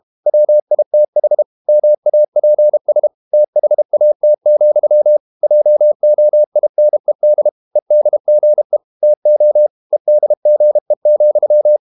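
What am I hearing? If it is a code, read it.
Morse code, heard as a keyed tone, about 32 wpm, 600 Hz. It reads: WITH MAPS THAT, JOINED EDGE TO EDGE,
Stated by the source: Morse code practice at 32 wpm from CW med Gnister, reading Ken Follett's novel Never in English.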